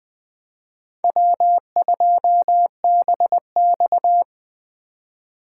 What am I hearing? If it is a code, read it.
Morse code, heard as a keyed tone, about 20 wpm, 700 Hz. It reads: W2BX
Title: Speed Racing - USA 1x2 Call Signs 17wpm